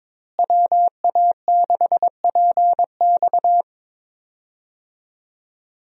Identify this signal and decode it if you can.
Morse code, heard as a keyed tone, about 22 wpm, 700 Hz. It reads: WA6PX
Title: USA Call Signs - 5 Characters - Repeated 3x - 22wpm